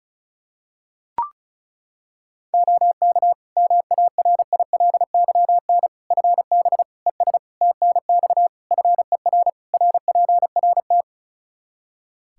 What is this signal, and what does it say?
Morse code, heard as a keyed tone, about 34 wpm, 700 Hz. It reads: OK MARILYN FB ES TNX FER RPRT
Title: Practice QSO - Exchange 2 - Info - 35wpm